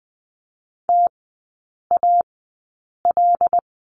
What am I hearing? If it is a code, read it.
Morse code, heard as a keyed tone, about 20 wpm, spaced as for 8 wpm, 700 Hz. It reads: TAL
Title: 074 Lesson 13e - Letter L - Three Characters 20wpm